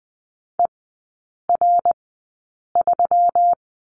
Morse code, heard as a keyed tone, about 20 wpm, spaced as for 8 wpm, 700 Hz. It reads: ER3
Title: 121 Lesson 20e - Number 3 - Three Characters 20wpm